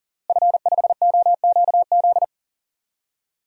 Morse code, heard as a keyed tone, about 39 wpm, 700 Hz. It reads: F5OQZ
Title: DXCC-100 Call Signs - 5 Characters - Repeated 2x - 40wpm